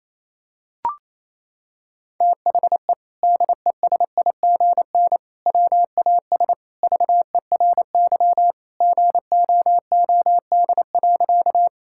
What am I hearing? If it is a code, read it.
Morse code, heard as a keyed tone, about 27 wpm, 700 Hz. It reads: THE DESIGN WAS VERY GOOD.
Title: Sentences from Top 1000 Words - Review Part 5 - 28wpm